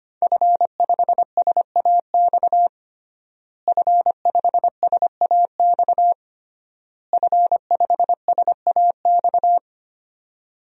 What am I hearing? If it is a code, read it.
Morse code, heard as a keyed tone, about 25 wpm, 700 Hz. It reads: F5SAX F5SAX F5SAX
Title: DXCC-100 Call Signs - 5 Characters - Repeated 3x - 25wpm